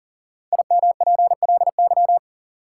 Morse code, heard as a keyed tone, about 40 wpm, 700 Hz. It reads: IMPLY
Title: Five Letter Words - 40wpm